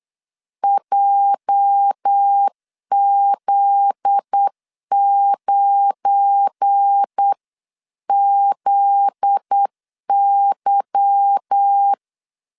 Morse code, heard as a keyed tone, about 8 wpm, 800 Hz. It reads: JZ9ZY
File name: LEZIONI / 10 SIGNS / 10A SLASH 8wpm 3452